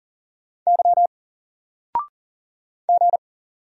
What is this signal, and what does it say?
Morse code, heard as a keyed tone, about 39 wpm, 700 Hz. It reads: Y G
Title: Single Character (Letter:Number:Punctuation:Prosign) 40wpm